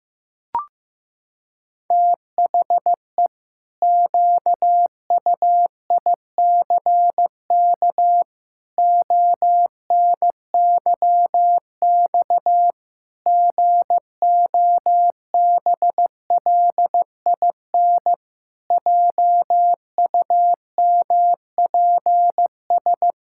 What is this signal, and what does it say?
Morse code, heard as a keyed tone, about 15 wpm, 700 Hz. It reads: THE QUICK ONYX GOBLIN JUMPS
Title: Pangram Challenge v2 - 15wpm